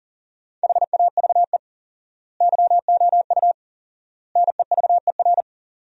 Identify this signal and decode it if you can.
Morse code, heard as a keyed tone, about 40 wpm, 700 Hz. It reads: HAVE YOU NEVER
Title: 4x Word-Spacing - Sentences from Top 300 Words - 40wpm